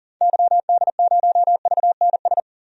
Morse code, heard as a keyed tone, about 40 wpm, 700 Hz. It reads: YD0VNS
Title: DXCC-100 Call Signs - 6 Characters - 40wpm